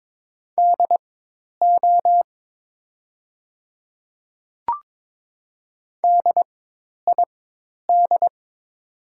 Morse code, heard as a keyed tone, about 22 wpm, spaced as for 10 wpm, 700 Hz. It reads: DO DID
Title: 4x Character Spacing - Top 100 Words - 22wpm